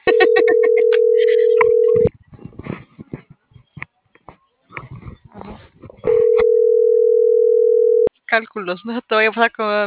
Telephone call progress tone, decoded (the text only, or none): ringback tone